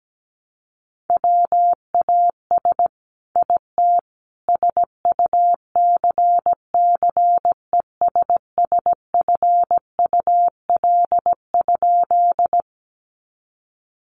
Morse code, heard as a keyed tone, about 17 wpm, 700 Hz. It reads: WAS IT SUCCESSFUL?
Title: A Sound Of Thunder 17wpm